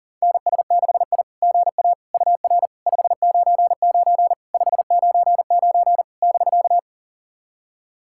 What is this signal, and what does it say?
Morse code, heard as a keyed tone, about 40 wpm, 700 Hz. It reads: NS6I GA UR 599 599 <BK>
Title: SOTA QSO Practice 40wpm